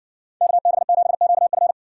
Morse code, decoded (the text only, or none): KB7QF